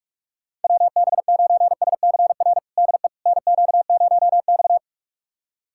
Morse code, heard as a keyed tone, about 42 wpm, 700 Hz. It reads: WB9SCR DE NQ0X
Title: USA Callsign DE Callsign 45wpm